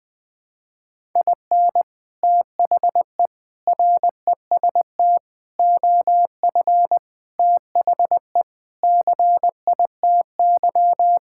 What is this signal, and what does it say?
Morse code, heard as a keyed tone, about 20 wpm, 700 Hz. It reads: IN THE REST OF THE CITY
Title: Sentences with 5-Word Phrases - Spoken Beginning - 20wpm